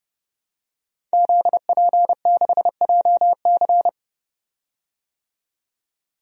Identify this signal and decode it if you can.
Morse code, heard as a keyed tone, about 32 wpm, 700 Hz. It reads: ZP6JC
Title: DXCC-100 Call Signs - 5 Characters - 30wpm